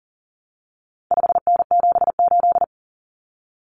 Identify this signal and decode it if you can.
Morse code, heard as a keyed tone, about 40 wpm, 700 Hz. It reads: 5N78
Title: Common Cut Numbers - 4-Digit 40wpm